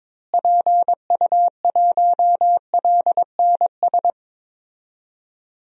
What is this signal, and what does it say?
Morse code, heard as a keyed tone, about 22 wpm, 700 Hz. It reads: PU1LNS
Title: DXCC-100 Call Signs - 6 Characters - Repeated 3x - 22wpm